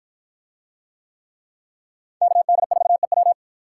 Morse code, heard as a keyed tone, about 49 wpm, 700 Hz. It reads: KD4EW